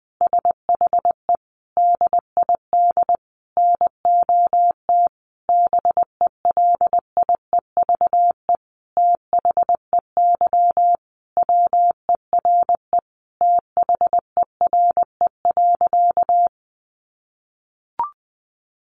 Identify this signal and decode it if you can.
Morse code, heard as a keyed tone, about 20 wpm, 700 Hz. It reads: SHE DID NOT BELIEVE THEY WERE THERE.
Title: Sentences from Top 1000 Words 20wpm